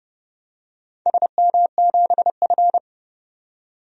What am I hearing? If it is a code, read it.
Morse code, heard as a keyed tone, about 30 wpm, 700 Hz. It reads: SM7F
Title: DXCC-100 Call Signs - 4 Characters - 30wpm